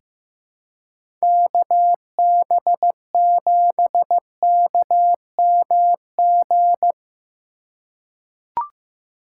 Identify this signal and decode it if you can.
Morse code, heard as a keyed tone, about 15 wpm, 700 Hz. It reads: KB7KMG
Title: USA Call Signs - 6 Characters - 4x Character Spacing - 15wpm